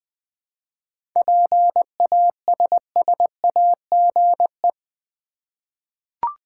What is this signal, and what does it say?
Morse code, heard as a keyed tone, about 20 wpm, 700 Hz. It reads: PASSAGE
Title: Seven Letter Words - 20wpm